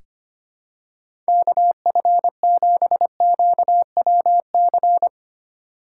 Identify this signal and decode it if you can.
Morse code, heard as a keyed tone, about 25 wpm, 700 Hz. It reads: KF7QWC
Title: USA Call Signs - 6 Characters - 3x Character Spacing - 25wpm